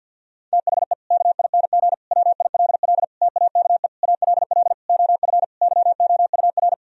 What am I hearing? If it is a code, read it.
Morse code, heard as a keyed tone, about 50 wpm, 700 Hz. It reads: THE KING WILL TAKE ALL OF YOUR
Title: Sentences from Top 1000 Words - Review Part 4 - 50wpm